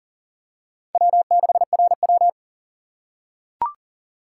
Morse code, heard as a keyed tone, about 42 wpm, 700 Hz. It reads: W6RW
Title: USA Call Signs - 4 Characters - 2x Character Spacing - 40wpm